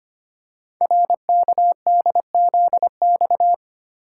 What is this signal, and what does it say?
Morse code, heard as a keyed tone, about 25 wpm, 700 Hz. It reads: RKDZX
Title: Code Groups - 5 Letters - 25wpm